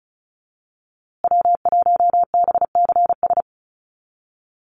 Morse code, heard as a keyed tone, about 34 wpm, 700 Hz. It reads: W1BCS
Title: USA Call Signs Repeated 3x 35wpm